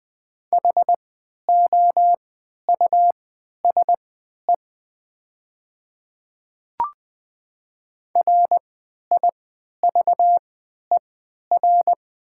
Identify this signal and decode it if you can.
Morse code, heard as a keyed tone, about 20 wpm, spaced as for 11 wpm, 700 Hz. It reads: HOUSE RIVER
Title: 3x Character Spacing - Top 300 Words - 20wpm